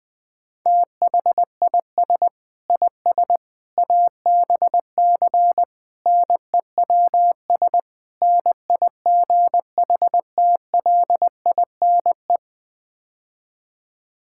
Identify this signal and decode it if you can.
Morse code, heard as a keyed tone, about 20 wpm, 700 Hz. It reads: THIS IS ABC NEWS NIGHTLINE
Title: Build A Phrase - 5-Word Phrases - 20wpm